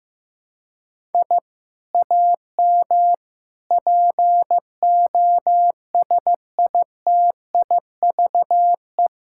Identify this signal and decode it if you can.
Morse code, heard as a keyed tone, about 15 wpm, 700 Hz. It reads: I AM POSITIVE
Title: Sentences from Top 1000 Words - Review Part 5 - 15wpm